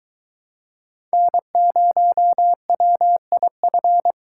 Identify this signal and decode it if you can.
Morse code, heard as a keyed tone, about 22 wpm, 700 Hz. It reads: N0WIF